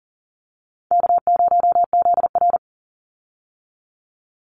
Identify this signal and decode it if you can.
Morse code, heard as a keyed tone, about 40 wpm, 700 Hz. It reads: K0ZR
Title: Easy Call Signs Repeated 2x - 40wpm